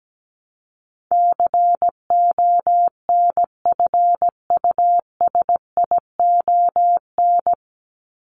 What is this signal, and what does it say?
Morse code, heard as a keyed tone, about 17 wpm, 700 Hz. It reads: CONFUSION
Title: Top 1501-1750 Words 17wpm